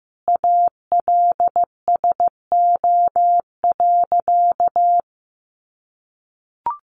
Morse code, heard as a keyed tone, about 15 wpm, 700 Hz. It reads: ALSO.
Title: Sentences from Top 1200 Words 15wpm